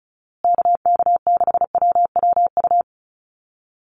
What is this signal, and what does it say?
Morse code, heard as a keyed tone, about 34 wpm, 700 Hz. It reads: KK6WWU
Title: USA Call Signs Repeated 3x 35wpm